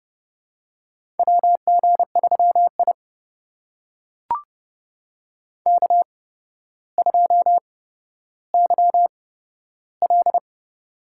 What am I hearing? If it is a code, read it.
Morse code, heard as a keyed tone, about 30 wpm, spaced as for 13 wpm, 700 Hz. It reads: WG3I K2YL
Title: USA Call Signs - 4 Characters - 8x Character Spacing - 30wpm